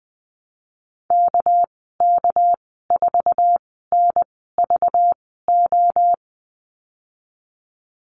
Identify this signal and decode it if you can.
Morse code, heard as a keyed tone, about 20 wpm, spaced as for 15 wpm, 700 Hz. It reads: KK4NVO
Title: 207 Lesson 33g - Letter j - Call Signs 20wpm